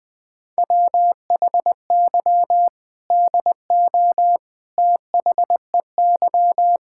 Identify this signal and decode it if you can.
Morse code, heard as a keyed tone, about 20 wpm, 700 Hz. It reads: WHY DO THEY
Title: Sentences from Top 1600 Words 20wpm